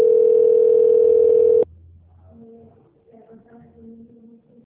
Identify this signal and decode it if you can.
Ringback tone (North America), 440 + 480 Hz.